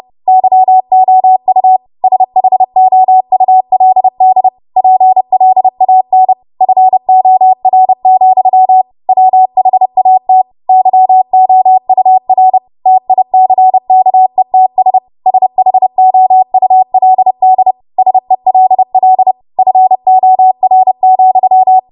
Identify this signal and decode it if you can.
Morse code, heard as a keyed tone, about 30 wpm, 750 Hz. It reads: YOU SHOULD PLAN FOR, WHAT YOUR TICKETS SHOULD SELL FOR,
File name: Morse / 30WPM